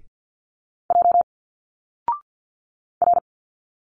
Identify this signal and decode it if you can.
Morse code, heard as a keyed tone, about 51 wpm, 700 Hz. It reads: J R
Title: Single Letters 50wpm